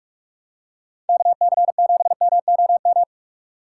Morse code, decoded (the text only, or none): KC7MOM